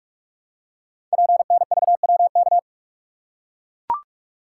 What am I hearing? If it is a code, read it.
Morse code, heard as a keyed tone, about 44 wpm, 690 Hz. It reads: PNVWK
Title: Code Groups - 5 Letters - 45wpm